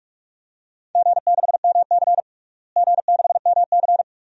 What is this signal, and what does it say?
Morse code, heard as a keyed tone, about 41 wpm, 700 Hz. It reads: G6MC G6MC
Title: DXCC-100 Call Signs - 4 Characters - Repeated 2x - 45wpm